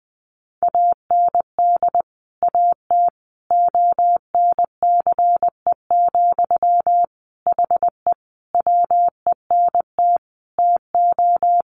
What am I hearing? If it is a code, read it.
Morse code, heard as a keyed tone, about 20 wpm, 700 Hz. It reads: AND AT ONCE, HE WENT TO
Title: Sentences from Top 1300 Words 20wpm